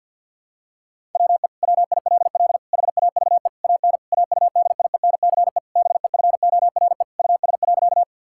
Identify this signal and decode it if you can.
Morse code, heard as a keyed tone, about 53 wpm, 690 Hz. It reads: WE WILL HAVE AN AUDIENCE BEFORE US.
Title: Sentences from Top 1000 Words - Review Part 5 - 50wpm